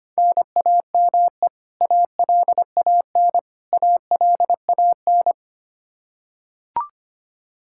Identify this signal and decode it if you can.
Morse code, heard as a keyed tone, about 25 wpm, 700 Hz. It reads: NAME ALAN ALAN
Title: Practice QSO - Exchange 1 - Beginning - 25wpm